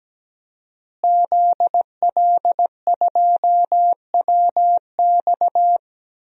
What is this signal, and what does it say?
Morse code, heard as a keyed tone, about 17 wpm, 700 Hz. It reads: ZL2WX
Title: DXCC-100 Call Signs - 5 Characters - Repeated 2x - 17wpm